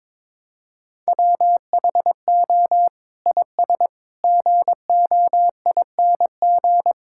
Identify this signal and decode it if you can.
Morse code, heard as a keyed tone, about 22 wpm, 700 Hz. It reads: WHO IS GOING
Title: Sentences from Top 1000 Words - Review Part 5 - 22wpm